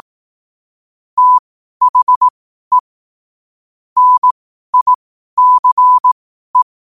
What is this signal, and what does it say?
Morse code, heard as a keyed tone, about 18 wpm, spaced as for 12 wpm, 1000 Hz. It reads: THE NICE
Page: Quotes for Wed, 13 Aug 2025 in Morse Code at 12 words per minute.